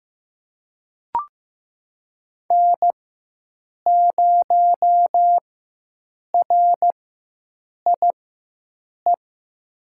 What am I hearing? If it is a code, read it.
Morse code, heard as a keyed tone, about 15 wpm, spaced as for 7 wpm, 700 Hz. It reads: N0RIE
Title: USA Call Signs - 5 Characters - 4x Character Spacing - 15wpm